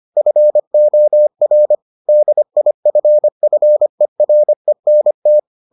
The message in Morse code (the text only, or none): FOR DIFFERENT